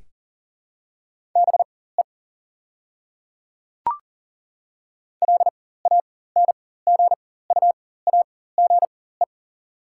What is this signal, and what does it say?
Morse code, heard as a keyed tone, about 39 wpm, spaced as for 19 wpm, 690 Hz. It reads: BE LANGUAGE
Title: 4x Character Spacing - Top 500 Words - 40wpm